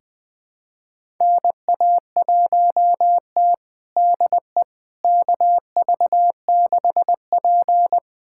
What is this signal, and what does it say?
Morse code, heard as a keyed tone, about 21 wpm, 700 Hz. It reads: NA1T DE KV6P
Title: USA Callsign DE Callsign - 4 Characters - 20wpm